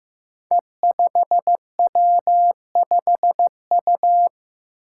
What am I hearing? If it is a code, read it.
Morse code, heard as a keyed tone, about 15 wpm, 700 Hz. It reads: E5W5U
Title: Five Challenging Letter-Number Combinations 15wpm